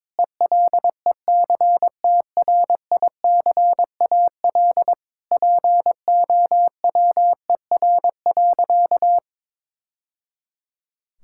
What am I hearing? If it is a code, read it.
Morse code, heard as a keyed tone, about 21 wpm, 700 Hz. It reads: ELECTRICAL POWER.
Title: Sentences from Top 1000 Words - Review Part 6 - 22wpm